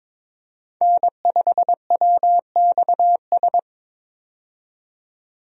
Morse code, heard as a keyed tone, about 22 wpm, 700 Hz. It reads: N5WXS
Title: USA Call Signs - 5 Characters - 22wpm